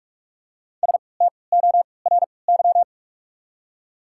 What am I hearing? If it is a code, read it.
Morse code, heard as a keyed tone, about 44 wpm, spaced as for 25 wpm, 690 Hz. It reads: STORY
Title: 3x Character Spacing - Top 300 Words - 45wpm